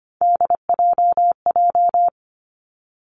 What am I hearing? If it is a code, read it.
Morse code, heard as a keyed tone, about 25 wpm, 700 Hz. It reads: DJJ
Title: Three Challenging Letter-Number Combinations 25wpm